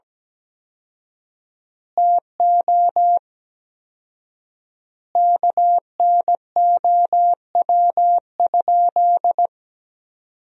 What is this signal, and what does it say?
Morse code, heard as a keyed tone, about 17 wpm, 700 Hz. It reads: TO KNOW?
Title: 4x Word-Spacing - Sentences from Top 200 Words - 17wpm